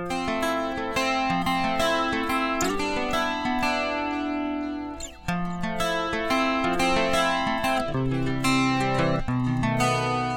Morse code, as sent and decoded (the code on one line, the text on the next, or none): none